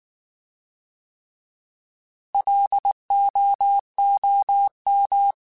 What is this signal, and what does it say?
Morse code, heard as a keyed tone, about 19 wpm, 800 Hz. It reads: LOOM
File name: Alphabetical / Alphabetical mp3s / Ls @18